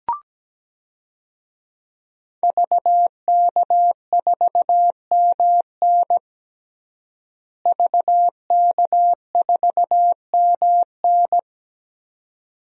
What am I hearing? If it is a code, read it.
Morse code, heard as a keyed tone, about 17 wpm, 700 Hz. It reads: VK4MN VK4MN
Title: DXCC-100 Call Signs - 5 Characters - Repeated 2x - 17wpm